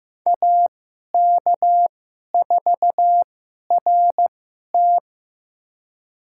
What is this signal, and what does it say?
Morse code, heard as a keyed tone, about 15 wpm, spaced as for 11 wpm, 700 Hz. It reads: AK4RT
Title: USA Call Signs - 5 Characters - 2x Character Spacing - 15wpm